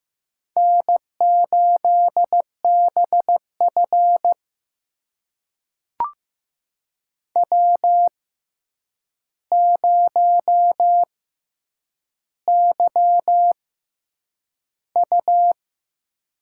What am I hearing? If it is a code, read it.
Morse code, heard as a keyed tone, about 15 wpm, spaced as for 8 wpm, 700 Hz. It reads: N8BF W0YU
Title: USA Call Signs - 4 Characters - 6x Character Spacing - 15wpm